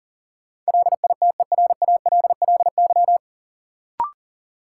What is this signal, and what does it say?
Morse code, heard as a keyed tone, about 40 wpm, 700 Hz. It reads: LITERALLY
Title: Sentences from Top 1400 Words - New Word Drill - 40wpm